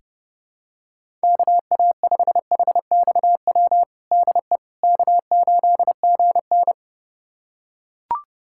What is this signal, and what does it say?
Morse code, heard as a keyed tone, about 30 wpm, 700 Hz. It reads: KA5HXW DE K8GN
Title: USA Callsign DE Callsign 30wpm